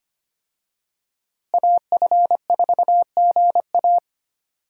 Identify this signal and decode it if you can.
Morse code, heard as a keyed tone, about 25 wpm, 700 Hz. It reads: AF4GA
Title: USA Call Signs - 5 Characters - Repeated 2x - 25wpm